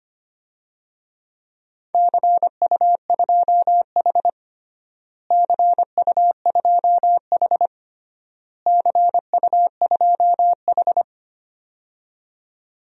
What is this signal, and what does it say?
Morse code, heard as a keyed tone, about 25 wpm, 700 Hz. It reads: CU2H CU2H CU2H
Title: DXCC-100 Call Signs - 4 Characters - Repeated 3x - 25wpm